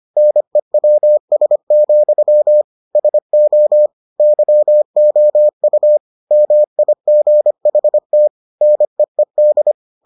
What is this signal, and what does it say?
Morse code, heard as a keyed tone, about 25 wpm, 600 Hz. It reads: NEWS, SO YOU MIGHT NEED